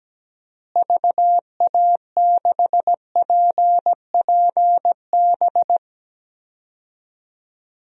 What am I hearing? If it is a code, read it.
Morse code, heard as a keyed tone, about 17 wpm, 700 Hz. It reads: VA6PPB
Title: DXCC-100 Call Signs - 6 Characters - Repeated 2x - 17wpm